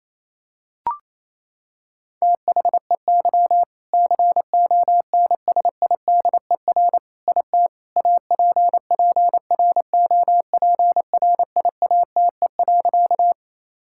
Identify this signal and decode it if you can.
Morse code, heard as a keyed tone, about 28 wpm, 700 Hz. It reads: THEY CONSIDER IT APPROPRIATE.
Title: Sentences from Top 1000 Words - Review Part 5 - 28wpm